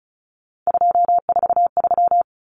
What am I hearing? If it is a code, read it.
Morse code, heard as a keyed tone, about 34 wpm, 700 Hz. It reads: 243
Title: Common Cut Numbers - 3-Digit 35wpm